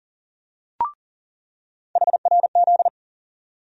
Morse code, heard as a keyed tone, about 40 wpm, 700 Hz. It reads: HRZ
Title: USA Call Sign Postfixes - 3 Letters 40wpm